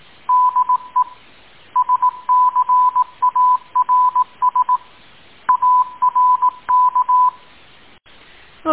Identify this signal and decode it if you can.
Morse code, heard as a keyed tone, about 18 wpm, 1000 Hz. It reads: DE SCARS ARK
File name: tin-upgrade / QSO 2020 09 01